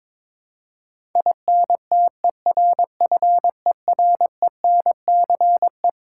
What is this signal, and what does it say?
Morse code, heard as a keyed tone, about 21 wpm, 700 Hz. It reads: INTERFERENCE
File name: Twelve Letter Words - 22wpm